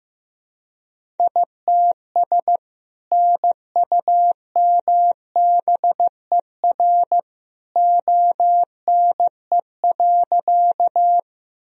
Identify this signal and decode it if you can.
Morse code, heard as a keyed tone, about 15 wpm, 700 Hz. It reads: ITS NUMBER ONE.
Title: Sentences from Top 1000 Words - Review Part 1 - 15wpm